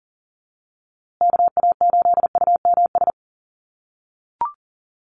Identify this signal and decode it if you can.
Morse code, heard as a keyed tone, about 40 wpm, 700 Hz. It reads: KA8UMS
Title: USA Call Signs Repeated 2x 40wpm